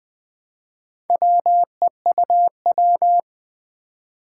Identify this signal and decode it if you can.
Morse code, heard as a keyed tone, about 21 wpm, 700 Hz. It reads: WEUW
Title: Four Challenging Letter-Number Combinations 20wpm